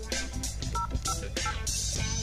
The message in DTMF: 058